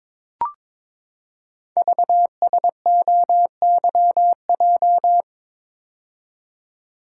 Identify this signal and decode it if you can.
Morse code, heard as a keyed tone, about 22 wpm, 700 Hz. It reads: VSOYJ
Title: Code Groups - 5 Letters - 22wpm